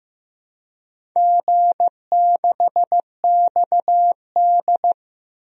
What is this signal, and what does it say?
Morse code, heard as a keyed tone, about 15 wpm, 700 Hz. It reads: G6XD
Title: DXCC-100 Call Signs - 4 Characters - Repeated 3x - 15wpm